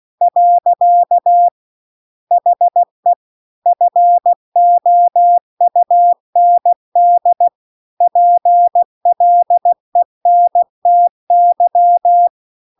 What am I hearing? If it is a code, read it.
Morse code, heard as a keyed tone, about 16 wpm, 700 Hz. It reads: . HE FOUND PLENTY
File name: 16WPM